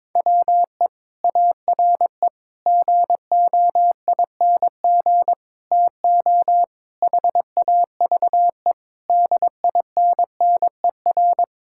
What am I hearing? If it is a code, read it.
Morse code, heard as a keyed tone, about 22 wpm, 700 Hz. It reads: WE ARE GOING TO HAVE DINNER
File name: practice / 22wpm / Sentences with 3-Word Phrases - 22wpm